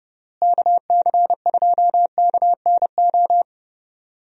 Morse code, heard as a keyed tone, about 32 wpm, 700 Hz. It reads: KC2KNO
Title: USA Call Signs - 6 Characters - Repeated 2x - 30wpm